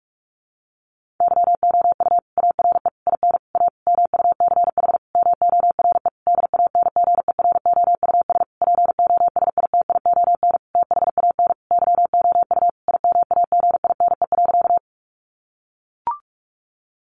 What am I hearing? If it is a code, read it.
Morse code, heard as a keyed tone, about 44 wpm, 700 Hz. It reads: YOU ARE IN A MUCH MORE DANGEROUS POSITION THAN YOU IMAGINE.
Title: Sentences from Top 1400 Words 45wpm